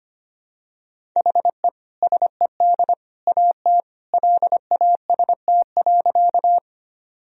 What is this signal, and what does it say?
Morse code, heard as a keyed tone, about 25 wpm, 700 Hz. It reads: HE SED AT LAST.
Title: Sentences from Top 300 Words and Abbreviations 25wpm